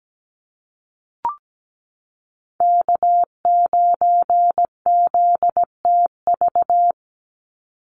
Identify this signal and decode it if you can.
Morse code, heard as a keyed tone, about 17 wpm, 700 Hz. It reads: K9ZTV
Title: Call Signs Repeated 2x - 17wpm